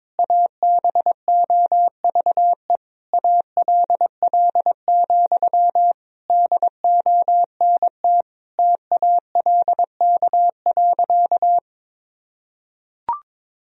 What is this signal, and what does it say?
Morse code, heard as a keyed tone, about 22 wpm, 700 Hz. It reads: ABOVE ALL, DONT TALK.
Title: Sentences from Top 1000 Words - Review Part 2 - 22wpm